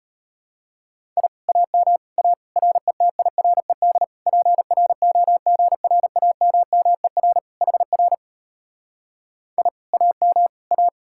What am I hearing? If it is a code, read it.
Morse code, heard as a keyed tone, about 33 wpm, 700 Hz. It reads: I AM A RETIRED PROGRAMMER HR I AM A